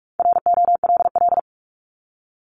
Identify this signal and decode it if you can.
Morse code, heard as a keyed tone, about 46 wpm, 700 Hz. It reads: ROLL
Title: Multi-letter Phonemes Repeated 3x 45wpm